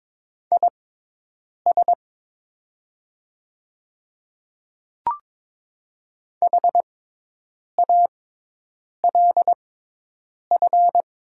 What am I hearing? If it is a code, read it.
Morse code, heard as a keyed tone, about 22 wpm, spaced as for 8 wpm, 700 Hz. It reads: IS HALF
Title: 6x Character Spacing - Top 500 Words - 22wpm